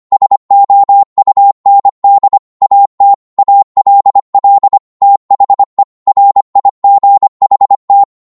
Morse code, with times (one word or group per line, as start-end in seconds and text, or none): 0.1-2.4 SOUND
2.6-3.1 AT
3.4-4.8 ALL
5.0-5.8 THE
6.1-8.0 RIGHT